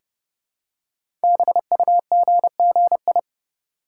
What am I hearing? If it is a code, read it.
Morse code, heard as a keyed tone, about 30 wpm, 700 Hz. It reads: BUGGI
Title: Code Groups - 5 Letter:Numbers - 30wpm